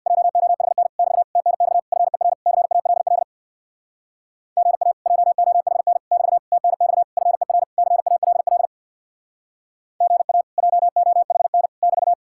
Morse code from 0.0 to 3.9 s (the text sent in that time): JOHN <BT> TNX FER CALL